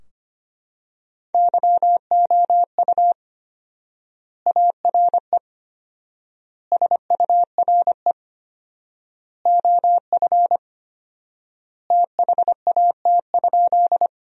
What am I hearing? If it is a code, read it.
Morse code, heard as a keyed tone, about 25 wpm, 700 Hz. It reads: YOU ARE SURE OF THAT?
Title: 4x Word-Spacing - Sentences from Top 300 Words - 25wpm